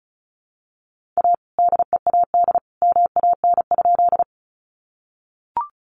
Morse code, heard as a keyed tone, about 34 wpm, 700 Hz. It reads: A DEAD MAN?
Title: Sentences from Top 1500 Words 35wpm